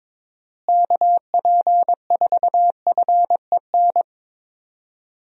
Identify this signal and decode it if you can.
Morse code, heard as a keyed tone, about 21 wpm, 700 Hz. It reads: KP4FEN
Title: DXCC-100 Call Signs - 6 Characters - Repeated 2x - 22wpm